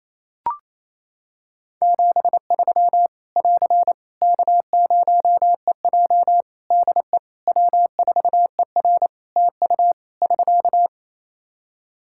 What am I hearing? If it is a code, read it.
Morse code, heard as a keyed tone, about 27 wpm, 700 Hz. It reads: 73 <AR> K0EJ DE W4ER TU <SK>
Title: Practice QSO - Exchange 4 - Ending - 28wpm